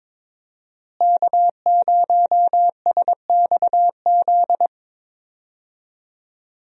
Morse code, heard as a keyed tone, about 22 wpm, 700 Hz. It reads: K0SXZ
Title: USA Call Signs - 5 Characters - Repeated 3x - 22wpm